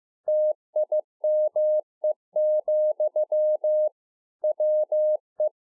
Morse code, heard as a keyed tone, about 15 wpm, 610 Hz. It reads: TIME, WE
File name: 2011-04-mp3 / THE 3.11 6 15WPM